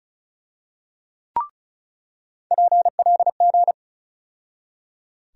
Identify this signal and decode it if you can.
Morse code, heard as a keyed tone, about 36 wpm, 700 Hz. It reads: PLG
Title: USA Call Sign Postfixes - 3 Letters 35wpm